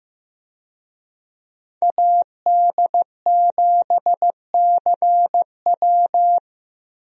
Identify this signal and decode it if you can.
Morse code, heard as a keyed tone, about 15 wpm, 700 Hz. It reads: AD7CW
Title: USA Call Signs - 5 Characters - Repeated 3x - 15wpm